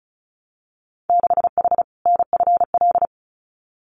Morse code, heard as a keyed tone, about 36 wpm, 700 Hz. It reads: 6H NFL
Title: Winter Field Day Exchanges Encore 35wpm